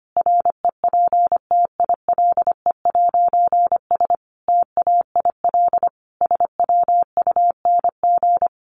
Morse code, heard as a keyed tone, about 25 wpm, 700 Hz. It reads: REPTILE'S TAIL SWUNG
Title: A Sound Of Thunder 25wpm